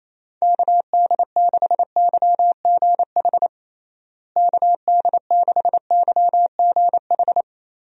Morse code, heard as a keyed tone, about 28 wpm, 700 Hz. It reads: KD6YGH KD6YGH
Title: USA Call Signs - 6 Characters - Repeated 2x - 28wpm